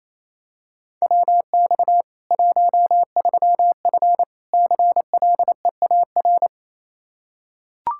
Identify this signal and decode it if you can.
Morse code, heard as a keyed tone, about 28 wpm, 700 Hz. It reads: WX 13F CLEAR
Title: K1USN SST QSO Practice 28wpm